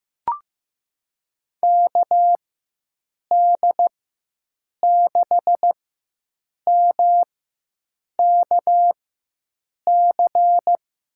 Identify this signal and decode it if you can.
Morse code, heard as a keyed tone, about 15 wpm, spaced as for 7 wpm, 700 Hz. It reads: KD6MKC
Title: USA Call Signs - 6 Characters - 4x Character Spacing - 15wpm